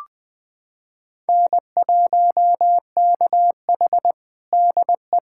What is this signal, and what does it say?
Morse code, heard as a keyed tone, about 21 wpm, 700 Hz. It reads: N1KH DE
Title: USA Callsign DE Callsign - 4 Characters - 20wpm